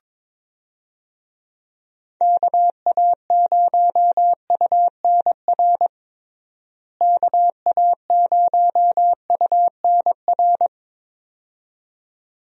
Morse code, heard as a keyed tone, about 22 wpm, 700 Hz. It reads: KA0UNR KA0UNR
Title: USA Call Signs - 6 Characters - Repeated 2x - 22wpm